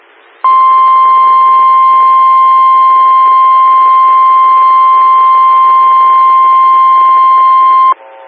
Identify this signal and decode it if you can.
Emergency Alert System (SAME) tones (the 1050 Hz attention tone). It is heard from 0.4 to 8.0 s.
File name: WX9MCS 147255 MHz SKYWARN Repeater